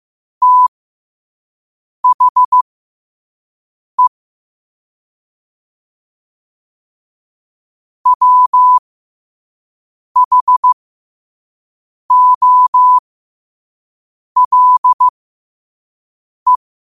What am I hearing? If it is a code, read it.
Morse code, heard as a keyed tone, about 15 wpm, spaced as for 5 wpm, 1000 Hz. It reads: THE WHOLE
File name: qotdfiles / 5wpm